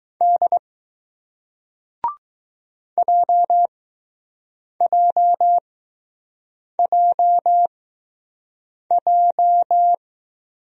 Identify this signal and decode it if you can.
Morse code, heard as a keyed tone, about 20 wpm, 700 Hz. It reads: D J J J J